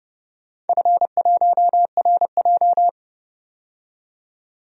Morse code, heard as a keyed tone, about 30 wpm, 700 Hz. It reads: F1RJ
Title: DXCC-100 Call Signs - 4 Characters - 30wpm